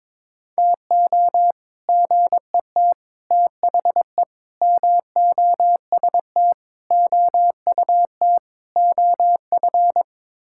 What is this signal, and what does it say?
Morse code, heard as a keyed tone, about 22 wpm, 700 Hz. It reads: TO GET THE MOST OUT OF
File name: practice / 22wpm / Sets of 6 Words - 22wpm